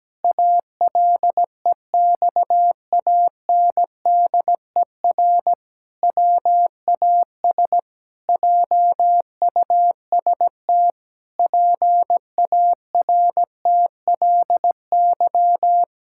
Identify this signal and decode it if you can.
Morse code, heard as a keyed tone, about 17 wpm, 700 Hz. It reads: ALEXANDER WAS JUST PARTLY
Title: Pangram Challenge v2 - 17wpm